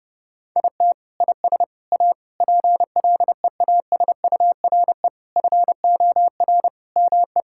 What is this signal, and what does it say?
Morse code, heard as a keyed tone, about 30 wpm, 700 Hz. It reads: IT IS A PLEASURE FOR ME
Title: Sentences from Top 1600 Words 30wpm